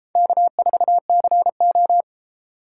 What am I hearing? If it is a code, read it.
Morse code, heard as a keyed tone, about 31 wpm, 700 Hz. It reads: K4CO